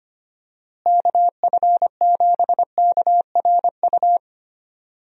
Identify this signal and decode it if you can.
Morse code, heard as a keyed tone, about 25 wpm, 700 Hz. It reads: KF7KRU
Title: USA Call Signs - 6 Characters - Repeated 3x - 25wpm